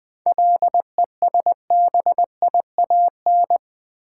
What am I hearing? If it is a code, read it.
Morse code, heard as a keyed tone, about 20 wpm, 700 Hz. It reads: LESBIAN
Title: 7 Letter Words - First 3 Letters Given - 20wpm